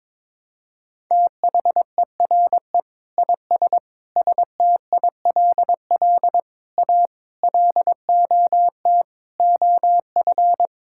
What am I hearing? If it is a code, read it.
Morse code, heard as a keyed tone, about 22 wpm, 700 Hz. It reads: THERE IS STILL A LOT OF
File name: Sentences with 9-Word Phrases - 22wpm